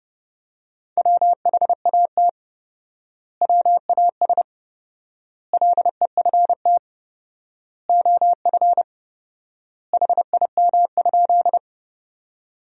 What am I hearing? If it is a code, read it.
Morse code, heard as a keyed tone, about 32 wpm, 700 Hz. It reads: WHAT WAS LEFT OF HIM?
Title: 4x Word-Spacing - Sentences from Top 300 Words - 30wpm